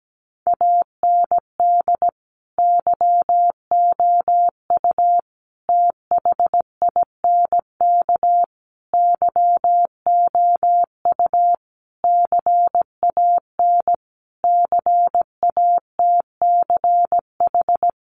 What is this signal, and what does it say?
Morse code, heard as a keyed tone, about 17 wpm, 700 Hz. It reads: AND YOU THINK YOU CAN CATCH
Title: Sentences from Top 1500 Words 17wpm